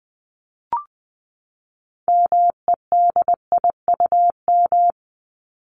Morse code, heard as a keyed tone, about 20 wpm, 700 Hz. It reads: MEDIUM
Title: Top 751-1000 Words 20wpm